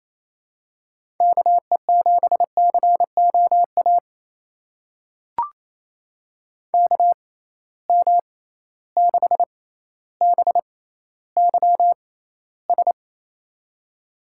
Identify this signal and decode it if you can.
Morse code, heard as a keyed tone, about 28 wpm, spaced as for 14 wpm, 700 Hz. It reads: KE7COA KM6BYS
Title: USA Call Signs - 6 Characters - 6x Character Spacing - 28wpm